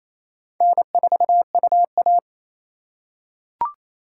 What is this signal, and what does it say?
Morse code, heard as a keyed tone, about 28 wpm, 700 Hz. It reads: N4UA
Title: USA Call Signs - 4 Characters - 3x Character Spacing - 28wpm